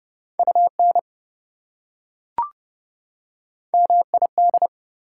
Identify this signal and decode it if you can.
Morse code, heard as a keyed tone, about 30 wpm, 700 Hz. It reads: UN MID
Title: Prefixes and Suffixes 30wpm